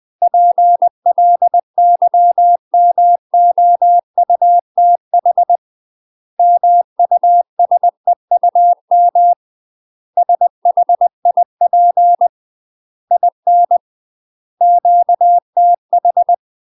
Morse code, heard as a keyed tone, about 20 wpm, 700 Hz. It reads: PLYMOUTH MUSEUM SHIP IN QTH
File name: UK MASTER QSOs TRAINING / 20WPM / UK MASTER QSOs 0030